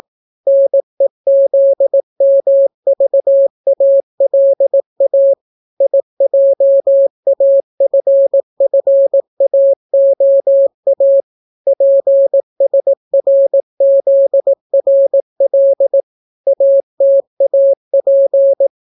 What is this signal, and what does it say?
Morse code, heard as a keyed tone, about 18 wpm, 550 Hz. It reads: NEZMVALA IJAFFAOA PSRZRL ATAP